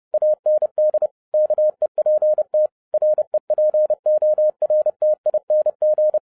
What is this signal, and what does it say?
Morse code, heard as a keyed tone, about 30 wpm, 610 Hz. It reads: AND KEPT REPORTING